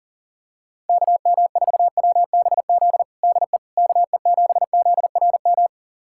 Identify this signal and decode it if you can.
Morse code, heard as a keyed tone, about 40 wpm, 700 Hz. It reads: KM4WBZ DE KE7ZRM
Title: USA Callsign DE Callsign 40wpm